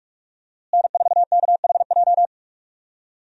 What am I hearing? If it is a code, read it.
Morse code, heard as a keyed tone, about 47 wpm, 700 Hz. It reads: N4KHJ
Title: USA Call Signs - 5 Characters - Repeated 2x - 45wpm